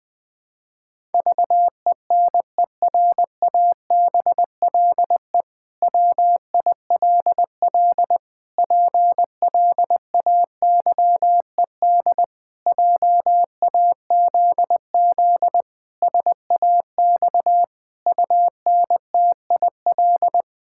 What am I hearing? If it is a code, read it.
Morse code, heard as a keyed tone, about 20 wpm, 700 Hz. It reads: VENERABLE WILL PLAYED JAZZ SAX UNTIL
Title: Pangram Challenge v2 - 20wpm